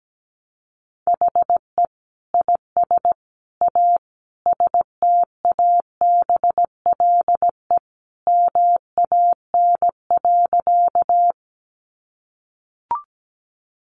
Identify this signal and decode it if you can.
Morse code, heard as a keyed tone, about 17 wpm, 700 Hz. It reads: HE IS A STABLE MAN.
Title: Sentences from Top 1500 Words 17wpm